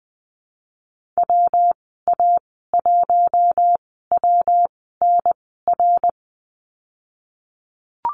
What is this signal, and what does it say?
Morse code, heard as a keyed tone, about 20 wpm, spaced as for 15 wpm, 700 Hz. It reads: WA1WNR
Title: 122 Lesson 20g - Number 3 - Call Signs 20wpm